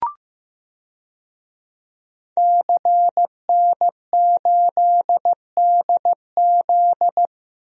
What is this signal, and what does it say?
Morse code, heard as a keyed tone, about 15 wpm, 700 Hz. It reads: CN8DZ
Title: DXCC-100 Call Signs - 5 Characters - 15wpm